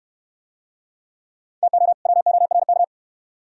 Morse code, heard as a keyed tone, about 66 wpm, 700 Hz. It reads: TO WORK